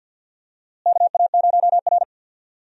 Morse code, heard as a keyed tone, about 52 wpm, 700 Hz. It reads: KA0R
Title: USA Call Signs - 4 Characters - Repeated 3x - 50wpm